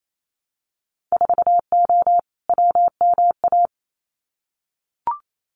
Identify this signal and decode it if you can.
Morse code, heard as a keyed tone, about 28 wpm, 690 Hz. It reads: 4O WMA
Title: Winter Field Day Exchanges Encore 28wpm